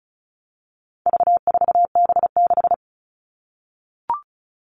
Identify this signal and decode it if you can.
Morse code, heard as a keyed tone, about 34 wpm, 700 Hz. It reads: V4B6
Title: All Cut Numbers - 4-Digit 35wpm